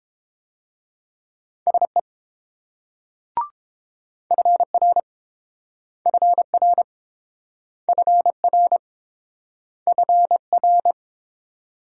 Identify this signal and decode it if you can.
Morse code, heard as a keyed tone, about 29 wpm, 690 Hz. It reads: SE FR FR FR FR